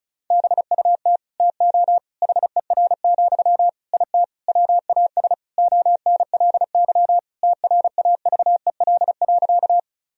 Morse code, heard as a keyed tone, about 34 wpm, 700 Hz. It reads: BUT TO HER, IT WAS ONLY TRAVEL.
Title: Sentences from Top 1000 Words - Review Part 2 - 35wpm